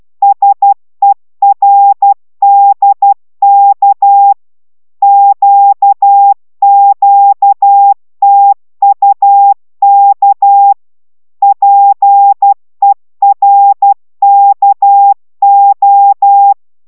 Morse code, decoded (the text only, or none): SERDK QQTUK PERKO